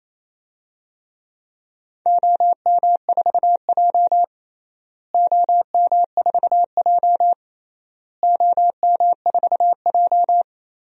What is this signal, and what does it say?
Morse code, heard as a keyed tone, about 28 wpm, 700 Hz. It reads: OM4J OM4J OM4J
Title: DXCC-100 Call Signs - 4 Characters - Repeated 3x - 28wpm